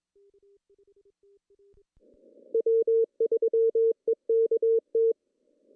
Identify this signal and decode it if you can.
Morse code, heard as a keyed tone, about 21 wpm, 450 Hz. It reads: W3EKT
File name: CW / cqww-cw-k5tr